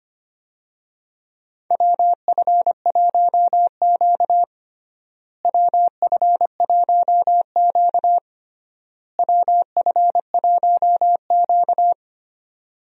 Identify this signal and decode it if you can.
Morse code, heard as a keyed tone, about 25 wpm, 700 Hz. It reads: WF1Q WF1Q WF1Q
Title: USA Call Signs - 4 Characters - Repeated 3x - 25wpm